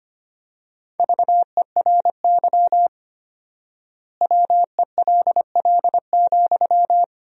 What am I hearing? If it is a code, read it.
Morse code, heard as a keyed tone, about 25 wpm, 700 Hz. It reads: VERY WELL,
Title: 4x Word-Spacing - Sentences from Top 200 Words - 25wpm